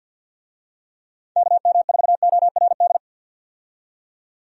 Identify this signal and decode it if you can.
Morse code, heard as a keyed tone, about 52 wpm, 700 Hz. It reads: KM4ORD
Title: USA Call Signs - 6 Characters - 50wpm